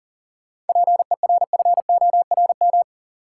Morse code, heard as a keyed tone, about 38 wpm, 700 Hz. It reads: PERFORM
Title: Sentences from Top 1000 Words - Review Part 5 - New Word Drill - 40wpm